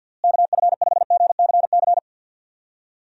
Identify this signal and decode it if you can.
Morse code, heard as a keyed tone, about 52 wpm, 700 Hz. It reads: KF5GCC
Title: USA Call Signs - 6 Characters - Repeated 2x - 50wpm